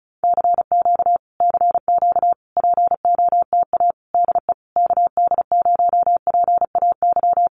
Morse code, heard as a keyed tone, about 34 wpm, 700 Hz. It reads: CQ CQ POTA DE KD0PAY
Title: Common QSO Phrases Encore 35wpm